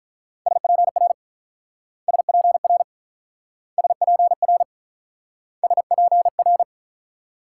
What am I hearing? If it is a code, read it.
Morse code, heard as a keyed tone, about 44 wpm, 700 Hz. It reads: SPR SPR SPR SPR